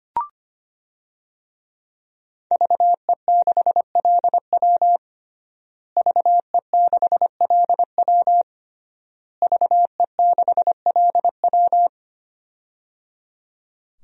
Morse code, heard as a keyed tone, about 25 wpm, 700 Hz. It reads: VE6LW VE6LW VE6LW
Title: DXCC-100 Call Signs - 5 Characters - Repeated 3x - 25wpm